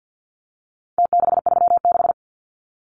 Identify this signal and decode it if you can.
Morse code, heard as a keyed tone, about 50 wpm, 700 Hz. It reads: T636
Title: Numbers - Contest Style - 4-Digit 50wpm